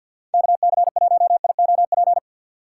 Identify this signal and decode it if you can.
Morse code, heard as a keyed tone, about 51 wpm, 700 Hz. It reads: KC1IOP
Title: USA Call Signs - 6 Characters - 4x Character Spacing - 50wpm